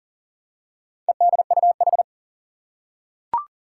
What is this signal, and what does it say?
Morse code, heard as a keyed tone, about 40 wpm, 700 Hz. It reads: EDUH
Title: Four Challenging Letter-Number Combinations 40wpm